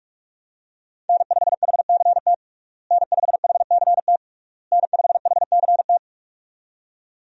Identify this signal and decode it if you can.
Morse code, heard as a keyed tone, about 46 wpm, 700 Hz. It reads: N5HCT N5HCT N5HCT
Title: USA Call Signs - 5 Characters - Repeated 3x - 45wpm